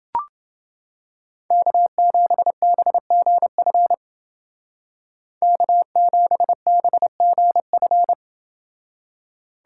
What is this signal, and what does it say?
Morse code, heard as a keyed tone, about 30 wpm, 700 Hz. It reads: K7BGF K7BGF